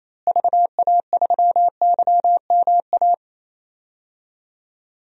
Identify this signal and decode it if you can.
Morse code, heard as a keyed tone, about 28 wpm, 700 Hz. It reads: VA3YMA
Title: DXCC-100 Call Signs - 6 Characters - Repeated 2x - 28wpm